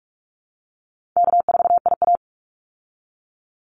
Morse code, heard as a keyed tone, about 44 wpm, 700 Hz. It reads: K4IA
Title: Easy Call Signs Repeated 2x - 45wpm